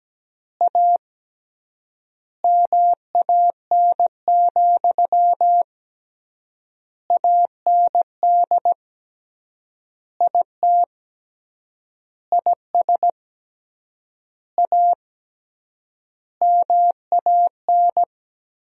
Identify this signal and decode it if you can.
Morse code, heard as a keyed tone, about 17 wpm, 700 Hz. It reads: A MAN, AND IT IS A MAN
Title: 3x Word-Spacing - Sentences from Top 200 Words - 17wpm